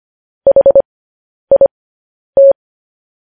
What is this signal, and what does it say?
Morse code, heard as a keyed tone, about 25 wpm, spaced as for 10 wpm, 550 Hz. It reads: HIT
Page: News Headlines in Morse Code at 10 WPM